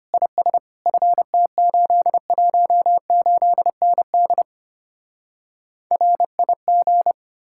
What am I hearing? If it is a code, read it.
Morse code, heard as a keyed tone, about 32 wpm, 700 Hz. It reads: IS FT818ND RIG